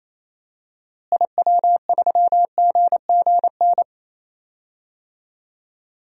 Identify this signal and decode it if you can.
Morse code, heard as a keyed tone, about 28 wpm, 700 Hz. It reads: IW3GGN
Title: DXCC-100 Call Signs - 6 Characters - Repeated 2x - 28wpm